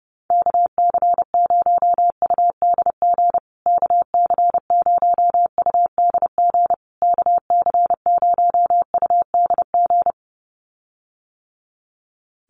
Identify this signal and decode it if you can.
Morse code, heard as a keyed tone, about 30 wpm, 700 Hz. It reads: KC0UDG KC0UDG KC0UDG
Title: USA Call Signs Repeated 3x 30wpm